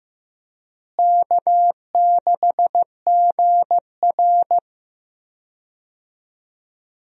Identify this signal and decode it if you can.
Morse code, heard as a keyed tone, about 15 wpm, 700 Hz. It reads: K6GR
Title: USA Call Signs - 4 Characters - Repeated 3x - 15wpm